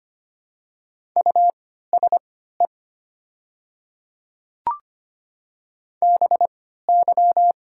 Morse code, heard as a keyed tone, about 25 wpm, spaced as for 14 wpm, 700 Hz. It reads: USE BY
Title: 3x Character Spacing - Top 500 Words - 25wpm